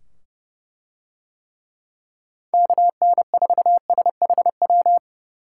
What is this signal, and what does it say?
Morse code, heard as a keyed tone, about 29 wpm, 700 Hz. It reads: KN4SHW